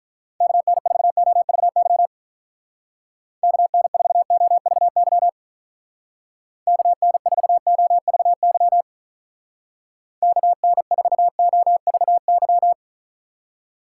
Morse code, heard as a keyed tone, about 44 wpm, 700 Hz. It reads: KN4OVY KN4OVY KN4OVY KN4OVY
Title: Speed Racing - USA 2x3 Call Signs 35wpm